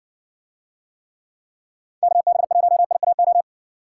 Morse code, heard as a keyed tone, about 56 wpm, 700 Hz. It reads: KB1IAO